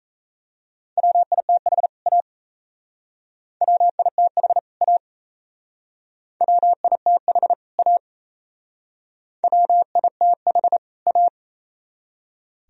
Morse code, heard as a keyed tone, about 34 wpm, 700 Hz. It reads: WITH A WITH A WITH A WITH A
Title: Speed-Racing - Sets of 2 Words 28wpm